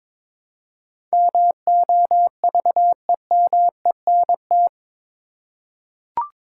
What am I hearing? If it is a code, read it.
Morse code, heard as a keyed tone, about 22 wpm, 700 Hz. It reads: MOVEMENT
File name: practice / 22wpm / Eight Letter Words - 22wpm